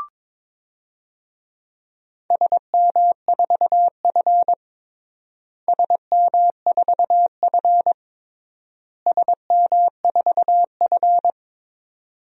Morse code, heard as a keyed tone, about 22 wpm, 700 Hz. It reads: SM4F SM4F SM4F
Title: DXCC-100 Call Signs - 4 Characters - Repeated 3x - 22wpm